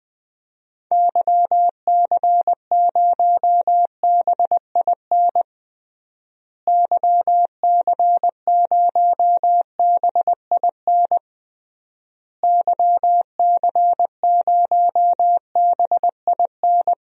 Morse code, heard as a keyed tone, about 20 wpm, 700 Hz. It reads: YC0BIN YC0BIN YC0BIN
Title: DXCC-100 Call Signs - 6 Characters - Repeated 3x - 20wpm